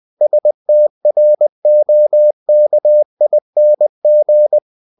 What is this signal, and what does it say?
Morse code, heard as a keyed tone, about 20 wpm, 600 Hz. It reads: STROKING
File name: cw-books / winnie-the-pooh / winnie-20wpm-page-0084